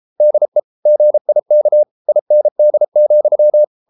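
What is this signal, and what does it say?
Morse code, heard as a keyed tone, about 33 wpm, 600 Hz. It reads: DE GIK IND,
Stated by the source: Morse code at 33 wpm, QRQ.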